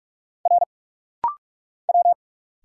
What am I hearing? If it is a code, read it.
Morse code, heard as a keyed tone, about 46 wpm, 690 Hz. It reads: R W
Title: Single Letters - Rapid-Fire 45wpm